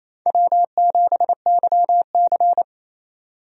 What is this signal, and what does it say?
Morse code, heard as a keyed tone, about 27 wpm, 700 Hz. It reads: W7YC
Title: USA Call Signs - 4 Characters - 28wpm